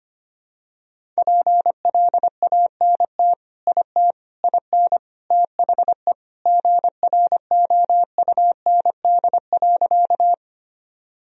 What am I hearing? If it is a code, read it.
Morse code, heard as a keyed tone, about 25 wpm, 700 Hz. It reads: PLANT IT IN THE GROUND.
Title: Sentences from Top 1000 Words - Review Part 1 - 25wpm